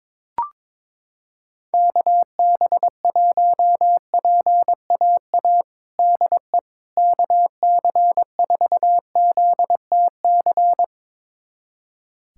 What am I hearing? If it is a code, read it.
Morse code, heard as a keyed tone, about 22 wpm, 700 Hz. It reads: KB1PAA DE KC4ZTC
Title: USA Callsign DE Callsign - 6 Characters - 22wpm